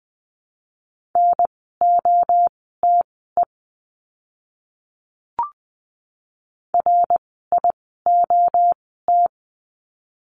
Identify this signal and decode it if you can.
Morse code, heard as a keyed tone, about 20 wpm, spaced as for 15 wpm, 700 Hz. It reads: NOTE RIOT
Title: 061 Lesson 11f - Letter h - Words 20wpm